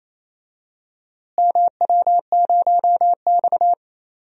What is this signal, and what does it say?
Morse code, heard as a keyed tone, about 27 wpm, 700 Hz. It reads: MW0X
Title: DXCC-100 Call Signs - 4 Characters - Repeated 2x - 28wpm